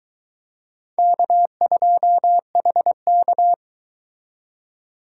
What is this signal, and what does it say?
Morse code, heard as a keyed tone, about 22 wpm, 700 Hz. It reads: K2HK